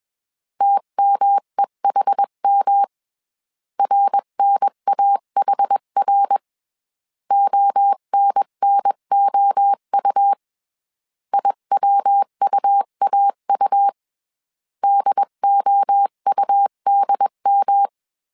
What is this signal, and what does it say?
Morse code, heard as a keyed tone, about 21 wpm, 800 Hz. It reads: TMEHM RNAHR ONNOU IWUAU DOUDM
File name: CW / DET / 1 3 1500C S B 20wpm 1754